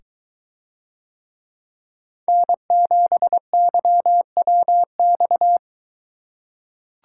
Morse code, heard as a keyed tone, about 23 wpm, 700 Hz. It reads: N7YWX